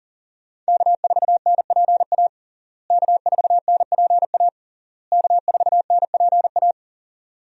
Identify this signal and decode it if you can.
Morse code, heard as a keyed tone, about 37 wpm, 700 Hz. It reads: K4NPA K4NPA K4NPA
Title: USA Call Signs - 5 Characters - Repeated 3x - 40wpm